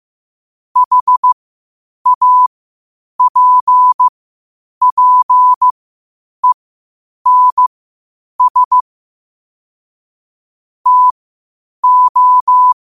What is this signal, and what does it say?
Morse code, heard as a keyed tone, about 15 wpm, spaced as for 8 wpm, 1000 Hz. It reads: HAPPENS TO
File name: qotdfiles / 8wpm